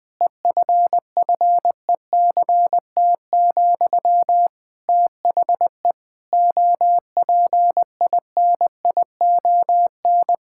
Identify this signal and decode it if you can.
Morse code, heard as a keyed tone, about 21 wpm, 700 Hz. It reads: EFFECT, THE OPINION
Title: Sentences from Top 1600 Words 20wpm